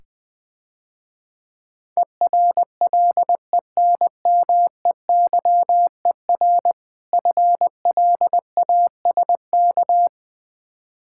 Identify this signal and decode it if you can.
Morse code, heard as a keyed tone, about 20 wpm, 700 Hz. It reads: ERLENMEYER FLASK
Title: Difficult Words - Speed-Racing with Hints - 20wpm